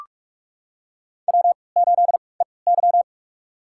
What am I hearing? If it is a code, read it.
Morse code, heard as a keyed tone, about 42 wpm, spaced as for 25 wpm, 700 Hz. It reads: W8EY
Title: USA Call Signs - 4 Characters - 3x Character Spacing - 45wpm